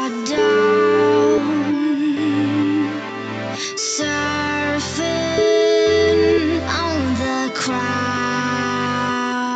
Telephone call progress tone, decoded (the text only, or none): ringback tone